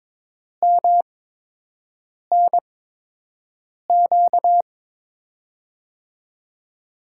Morse code, heard as a keyed tone, about 22 wpm, 700 Hz. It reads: M N Q
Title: USA Call Signs - 5 Characters - 8x Character Spacing - 22wpm